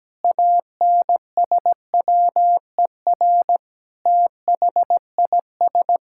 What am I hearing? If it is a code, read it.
Morse code, heard as a keyed tone, about 17 wpm, 700 Hz. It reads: ANSWER THIS
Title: Sentences from Top 1000 Words - Review Part 6 - 17wpm